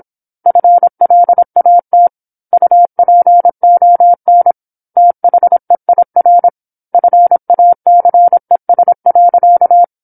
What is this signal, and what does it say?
Morse code, heard as a keyed tone, about 26 wpm, 700 Hz. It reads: FLAT UPON THEIR FACES.
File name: 08-Chapter-08 26 WPM